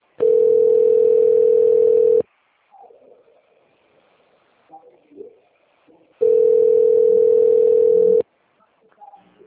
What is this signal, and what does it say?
Ringback tone (North America), 440 + 480 Hz.